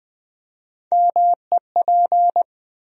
Morse code, heard as a keyed tone, about 20 wpm, 700 Hz. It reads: MEP
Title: USA Call Sign Postfixes - 3 Letters 20wpm